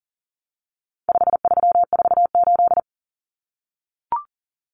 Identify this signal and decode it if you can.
Morse code, heard as a keyed tone, about 40 wpm, 700 Hz. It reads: 5348
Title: Common Cut Numbers - 4-Digit 40wpm